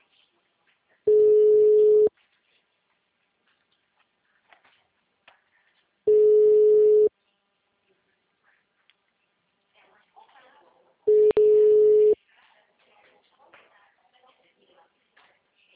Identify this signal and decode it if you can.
Ringback tone (Europe, 425 Hz), with speech.
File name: out- -2054- - - .167